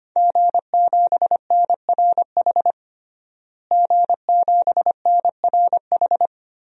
Morse code, heard as a keyed tone, about 25 wpm, 700 Hz. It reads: G7NRH G7NRH
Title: DXCC-100 Call Signs - 5 Characters - Repeated 3x - 25wpm